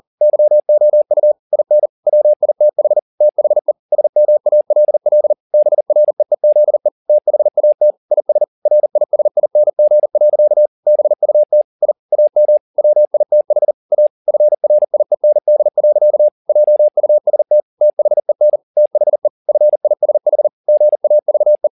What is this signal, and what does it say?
Morse code, heard as a keyed tone, about 40 wpm, 600 Hz. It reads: YOU IN WITH THE SMALL BREEZE THAT IS RISING. BUT I AM WITH A FRIEND. JUST THEN THE FISH GAVE